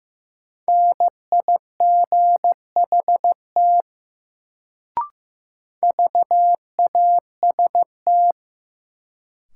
Morse code, heard as a keyed tone, about 15 wpm, 700 Hz. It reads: NIGHT VAST
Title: Sets of 4 Words - New Word Drill - 15wpm